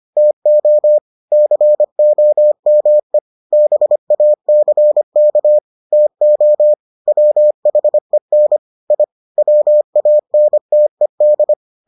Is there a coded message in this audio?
Yes: Morse code from 0.2 to 11.5 s, heard as a keyed tone, about 25 wpm, 600 Hz. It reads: TO COME BACK TO WHEN I WANTED